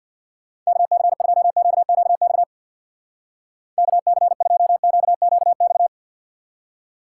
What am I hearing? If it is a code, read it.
Morse code, heard as a keyed tone, about 61 wpm, 700 Hz. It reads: KC2QQX KC2QQX
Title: Speed Racing - USA 2x3 Call Signs 50wpm